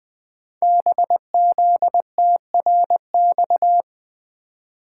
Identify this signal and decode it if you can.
Morse code, heard as a keyed tone, about 20 wpm, 700 Hz. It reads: BZTRX
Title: Code Groups - 5 Letter:Numbers - 20wpm